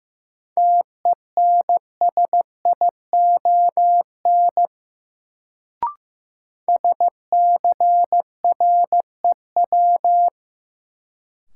Morse code, heard as a keyed tone, about 15 wpm, 700 Hz. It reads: TENSION SCREW
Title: Sentences from Top 1600 Words - New Word Drill Alt - 15wpm